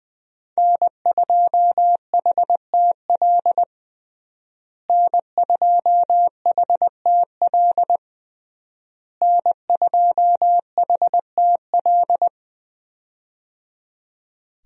Morse code, heard as a keyed tone, about 21 wpm, 700 Hz. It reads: N2HTL N2HTL N2HTL
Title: USA Call Signs - 5 Characters - Repeated 3x - 20wpm